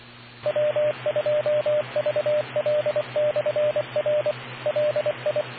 Morse code, heard as a keyed tone, about 25 wpm, 600 Hz, with radio noise. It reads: W2VL/R LI